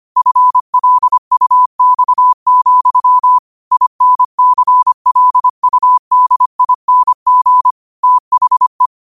Morse code, heard as a keyed tone, about 25 wpm, 1000 Hz. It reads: FLUX, INCLUDING THE